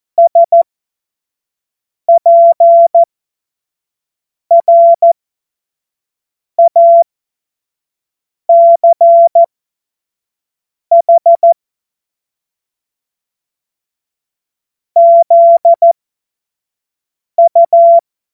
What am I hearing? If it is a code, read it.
Morse code, heard as a keyed tone, about 14 wpm, spaced as for 5 wpm, 680 Hz. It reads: SPRACH ZU
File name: Morsekurs / Witze / Witz 01. August 2023 25 F70